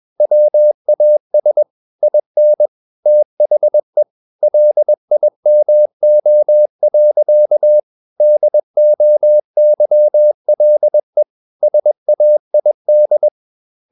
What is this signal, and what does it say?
Morse code, heard as a keyed tone, about 21 wpm, 600 Hz. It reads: WAS IN THE LIMO. DOYLE SAID
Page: CW med Gnister